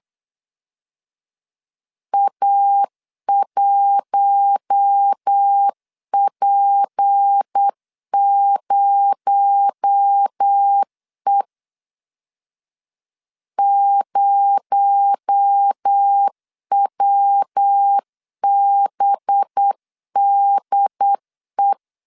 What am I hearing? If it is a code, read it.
Morse code, heard as a keyed tone, about 8 wpm, 800 Hz. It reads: A1P0E 0WBDE
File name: LEZIONI / 6 PKX0 / 6 PKX0 8wpm 3341